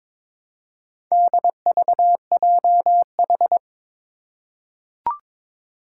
Morse code, heard as a keyed tone, about 22 wpm, 700 Hz. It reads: DVJH
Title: Four Challenging Letter-Number Combinations 22wpm